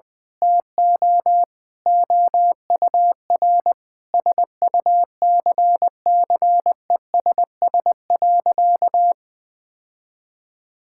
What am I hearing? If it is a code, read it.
Morse code, heard as a keyed tone, about 20 wpm, 700 Hz. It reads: TO OUR SUCCESS.
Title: Sentences with 9-Word Phrases - Spoken Beginning - 20wpm